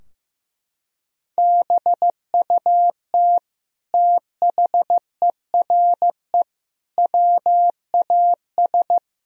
Morse code, heard as a keyed tone, about 16 wpm, 700 Hz. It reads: BUT THERE WAS
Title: Sentences from Top 1000 Words - Review Part 2 - 15wpm